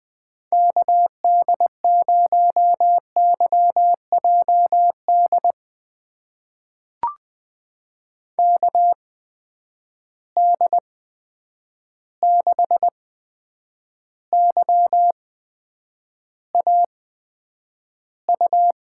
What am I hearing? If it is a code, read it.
Morse code, heard as a keyed tone, about 21 wpm, spaced as for 9 wpm, 700 Hz. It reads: KD0YJD KD6YAU
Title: USA Call Signs - 6 Characters - 8x Character Spacing - 20wpm